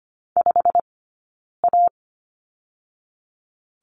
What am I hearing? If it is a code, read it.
Morse code, heard as a keyed tone, about 25 wpm, 700 Hz. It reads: 5 A
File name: Lesson 15d - Number 5 - Two Characters 25wpm